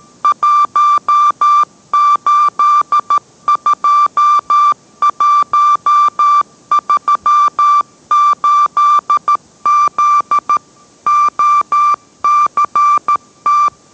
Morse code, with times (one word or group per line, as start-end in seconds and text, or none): 0.2-10.6 182138Z
11.1-13.7 OCT